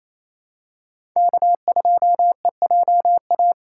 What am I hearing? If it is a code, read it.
Morse code, heard as a keyed tone, about 28 wpm, 700 Hz. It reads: K2EJA
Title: USA Call Signs - 5 Characters - 4x Character Spacing - 28wpm